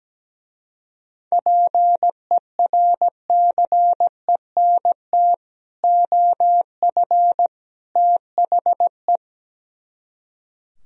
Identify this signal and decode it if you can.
Morse code, heard as a keyed tone, about 17 wpm, 700 Hz. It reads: PERCENT OF THE
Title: Speed-Racing - Sets of 3 Words 17wpm